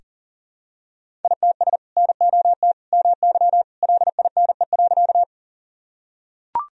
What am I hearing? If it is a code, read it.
Morse code, heard as a keyed tone, about 38 wpm, 700 Hz. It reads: ITS NOT MY LINE.
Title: Sentences from Top 1000 Words - Review Part 1 - 40wpm